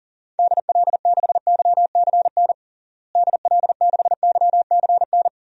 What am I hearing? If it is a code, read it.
Morse code, heard as a keyed tone, about 40 wpm, 700 Hz. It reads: DL6YCN DL6YCN
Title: DXCC-100 Call Signs - 6 Characters - Repeated 3x - 40wpm